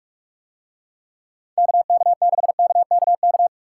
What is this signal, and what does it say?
Morse code, heard as a keyed tone, about 46 wpm, 700 Hz. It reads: KK6KKK